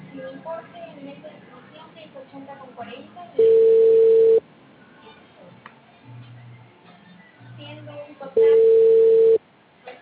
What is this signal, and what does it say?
Ringback tone (China, 450 Hz), with speech.